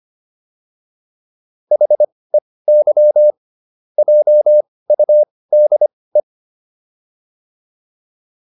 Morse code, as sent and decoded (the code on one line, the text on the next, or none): .... -.-- / .--- ..- -.. .
HEY JUDE